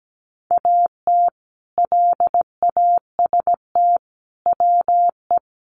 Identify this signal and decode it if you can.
Morse code, heard as a keyed tone, about 17 wpm, 700 Hz. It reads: AT LAST WE
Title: Sentences from Top 300 Words 17wpm